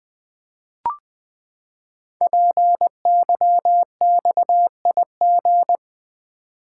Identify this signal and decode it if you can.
Morse code, heard as a keyed tone, about 20 wpm, 700 Hz. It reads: PYXIG
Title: Code Groups - 5 Letters - 20wpm